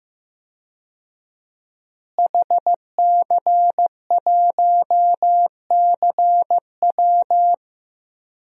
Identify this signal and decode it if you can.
Morse code, heard as a keyed tone, about 15 wpm, 700 Hz. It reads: HC1CW
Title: DXCC-100 Call Signs - 5 Characters - Repeated 3x - 15wpm